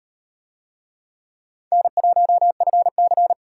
Morse code, heard as a keyed tone, about 35 wpm, 700 Hz. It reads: N1FC